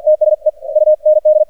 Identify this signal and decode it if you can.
Morse code, heard as a keyed tone, about 48 wpm, 610 Hz. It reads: TSE UND
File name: archived / CRSnet